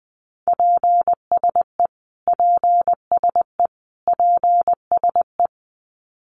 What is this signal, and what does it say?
Morse code, heard as a keyed tone, about 20 wpm, 700 Hz. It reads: PSE PSE PSE
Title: Abbreviations Repeated 3x - 20wpm